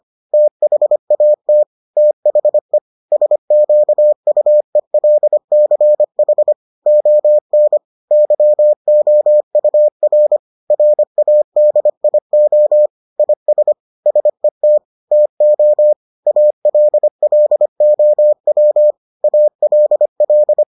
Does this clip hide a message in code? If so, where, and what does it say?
Yes: Morse code from 0.3 to 20.6 s, heard as a keyed tone, about 25 wpm, 600 Hz. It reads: THAT THE SQUELCH ON YOUR RADIO IS SET TO ALLOW ALL